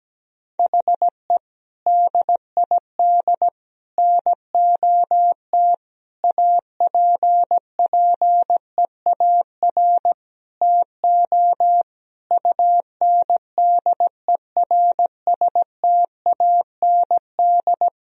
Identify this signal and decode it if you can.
Morse code, heard as a keyed tone, about 17 wpm, 700 Hz. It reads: HE DID NOT APPEAR TO UNDERSTAND
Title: Sentences from Top 1000 Words - Review Part 2 - 17wpm